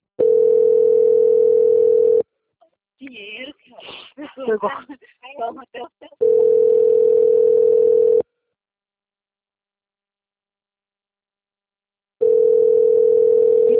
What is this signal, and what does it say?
Ringback tone (North America), 440 + 480 Hz, with speech.
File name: out- - - - - .28991